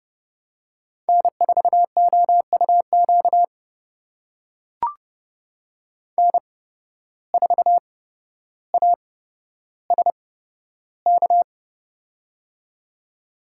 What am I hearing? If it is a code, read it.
Morse code, heard as a keyed tone, about 30 wpm, spaced as for 13 wpm, 700 Hz. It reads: N4OUQ N4ASK
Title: USA Call Signs - 5 Characters - 8x Character Spacing - 30wpm